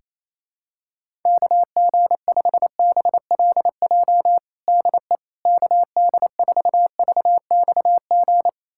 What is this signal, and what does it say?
Morse code, heard as a keyed tone, about 28 wpm, 700 Hz. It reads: KG5BLJ DE KD4VXG
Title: USA Callsign DE Callsign 28wpm